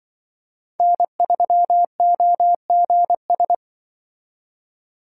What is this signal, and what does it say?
Morse code, heard as a keyed tone, about 25 wpm, 700 Hz. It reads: N3OGS
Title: Speed Racing - USA 1x3 Call Signs 20wpm